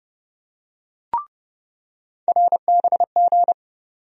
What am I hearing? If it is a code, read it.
Morse code, heard as a keyed tone, about 29 wpm, 700 Hz. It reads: RBG
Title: USA Call Sign Postfixes - 3 Letters 30wpm